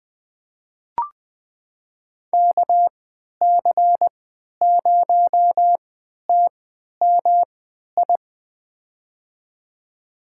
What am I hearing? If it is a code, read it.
Morse code, heard as a keyed tone, about 20 wpm, spaced as for 11 wpm, 700 Hz. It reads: KC0TMI
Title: USA Call Signs - 6 Characters - 3x Character Spacing - 20wpm